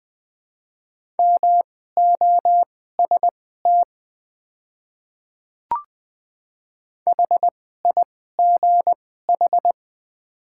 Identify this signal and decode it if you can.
Morse code, heard as a keyed tone, about 20 wpm, spaced as for 14 wpm, 690 Hz. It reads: MOST HIGH
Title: 2x Character Spacing - Top 200 Words - 20wpm